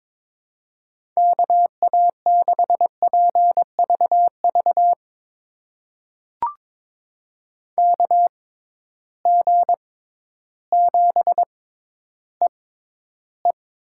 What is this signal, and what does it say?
Morse code, heard as a keyed tone, about 22 wpm, spaced as for 12 wpm, 700 Hz. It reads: KA6PVV KG7EE
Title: USA Call Signs - 6 Characters - 6x Character Spacing - 22wpm